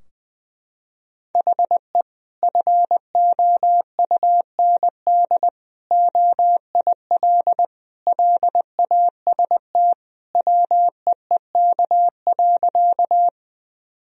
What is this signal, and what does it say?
Morse code, heard as a keyed tone, about 20 wpm, 700 Hz. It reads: HE FOUND OIL LAST WEEK.
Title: Sentences from Top 1000 Words - Review Part 4 - 20wpm